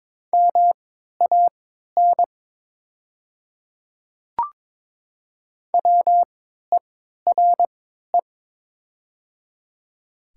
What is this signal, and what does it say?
Morse code, heard as a keyed tone, about 22 wpm, spaced as for 13 wpm, 700 Hz. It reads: MAN WERE
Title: 3x Character Spacing - Top 500 Words - 22wpm